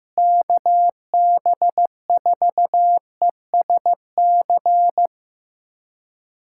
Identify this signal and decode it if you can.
Morse code, heard as a keyed tone, about 15 wpm, 700 Hz. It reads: KB4ESC
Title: USA Callsign DE Callsign 15wpm